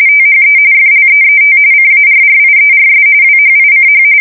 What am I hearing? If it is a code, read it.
RTTY, about 45.45 baud. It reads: THE FLEET CARRIERS WHICH
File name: fsk170 / USS-New-Jersey